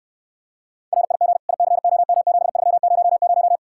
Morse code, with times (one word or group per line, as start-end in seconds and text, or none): 0.9-1.4 RIG
1.5-3.6 ICOM7300